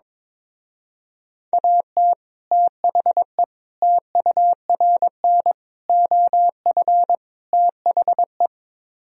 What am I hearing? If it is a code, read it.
Morse code, heard as a keyed tone, about 22 wpm, 700 Hz. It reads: AT THE TURN OF THE
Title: Speed-Racing - Sets of 5 Words 22wpm